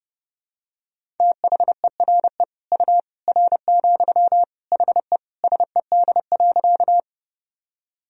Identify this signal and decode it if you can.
Morse code, heard as a keyed tone, about 30 wpm, 700 Hz. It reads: THERE U R, HE SED.
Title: Sentences from Top 100 Words and Abbreviations 30wpm